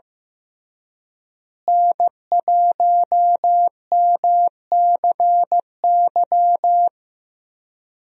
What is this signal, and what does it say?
Morse code, heard as a keyed tone, about 15 wpm, 700 Hz. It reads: N1MCY
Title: USA Call Signs - 5 Characters - Repeated 3x - 15wpm